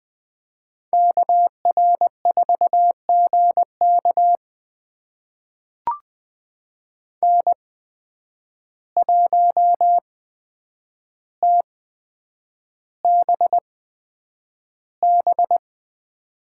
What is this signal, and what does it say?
Morse code, heard as a keyed tone, about 20 wpm, spaced as for 9 wpm, 700 Hz. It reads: KR4GK N1TBB
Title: USA Call Signs - 5 Characters - 8x Character Spacing - 20wpm